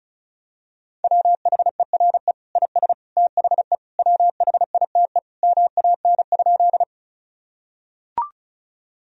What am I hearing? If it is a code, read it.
Morse code, heard as a keyed tone, about 34 wpm, 700 Hz. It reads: WHERE IS THE WHITE MAN?
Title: Sentences from Top 1000 Words - Review Part 1 - 35wpm